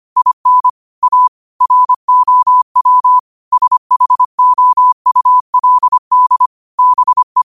Morse code, heard as a keyed tone, about 25 wpm, 1000 Hz. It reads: IN A ROW SHOULD BE
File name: qotdfiles / 25wpm